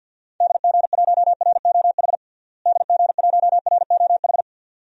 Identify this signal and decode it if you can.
Morse code, heard as a keyed tone, about 47 wpm, 700 Hz. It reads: DG1ROH DG1ROH
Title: DXCC-100 Call Signs - 6 Characters - Repeated 2x - 50wpm